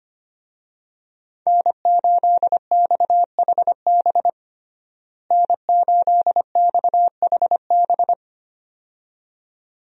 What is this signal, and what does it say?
Morse code, heard as a keyed tone, about 25 wpm, 700 Hz. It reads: N8XHB N8XHB
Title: USA Call Signs - 5 Characters - Repeated 2x - 25wpm